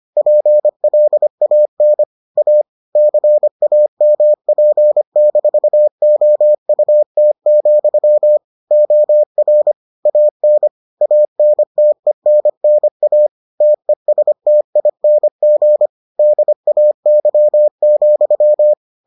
Morse code, heard as a keyed tone, about 25 wpm, 600 Hz. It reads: PLAN A CAMP-OUT, OR AN ANTENNA TESTING DAY,